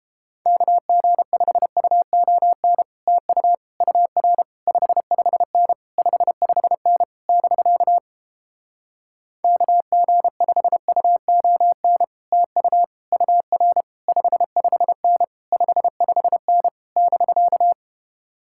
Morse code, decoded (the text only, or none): KG5UON TU UR 55N 55N <BK> KG5UON TU UR 55N 55N <BK>